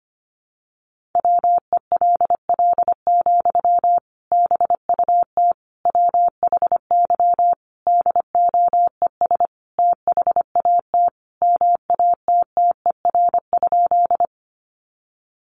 Morse code, heard as a keyed tone, about 25 wpm, 700 Hz. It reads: WELL, BUT WHY DOES THAT MATTER?
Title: Sentences from Top 700 Words 25wpm